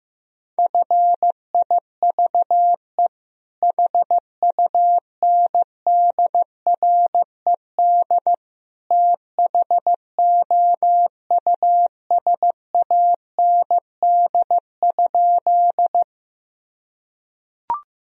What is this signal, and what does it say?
Morse code, heard as a keyed tone, about 15 wpm, 700 Hz. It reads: FIVE HUNDRED THOUSAND?
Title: Sentences from Top 1000 Words - Review Part 3 - 15wpm